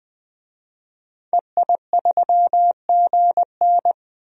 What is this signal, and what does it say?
Morse code, heard as a keyed tone, about 20 wpm, 700 Hz. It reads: EI3GN
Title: DXCC-100 Call Signs - 5 Characters - 20wpm